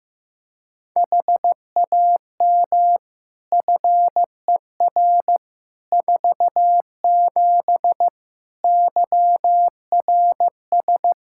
Morse code, heard as a keyed tone, about 16 wpm, 700 Hz. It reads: HAM FER 47 YRS
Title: K1USN SST QSO Practice 15wpm